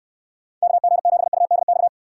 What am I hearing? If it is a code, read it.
Morse code, heard as a keyed tone, about 66 wpm, 700 Hz. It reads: KG7UDX